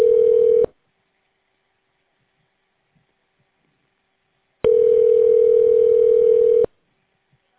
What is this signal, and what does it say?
Ringback tone (North America), 440 + 480 Hz.